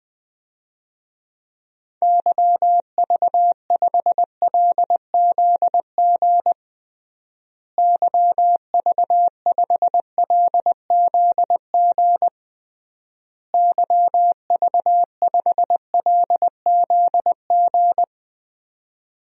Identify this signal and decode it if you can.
Morse code, heard as a keyed tone, about 21 wpm, 700 Hz. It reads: YV5LZG YV5LZG YV5LZG
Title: DXCC-100 Call Signs - 6 Characters - Repeated 3x - 20wpm